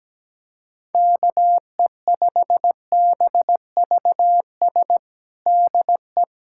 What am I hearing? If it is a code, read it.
Morse code, heard as a keyed tone, about 17 wpm, 700 Hz. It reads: KE5BVS DE
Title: USA Callsign DE Callsign - 6 Characters - 17wpm